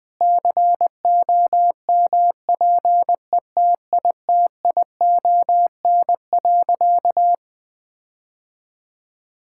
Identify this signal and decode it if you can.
Morse code, heard as a keyed tone, about 20 wpm, 700 Hz. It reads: COMPETITION.
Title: Sentences with 6-Word Phrases - 20wpm